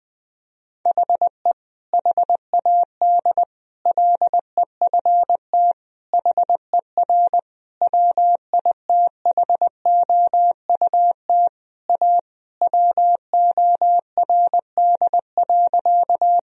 Morse code, heard as a keyed tone, about 20 wpm, 700 Hz. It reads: HE HAD LEFT HER WITHOUT A WORD.
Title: Sentences from Top 1000 Words - Review Part 3 - 20wpm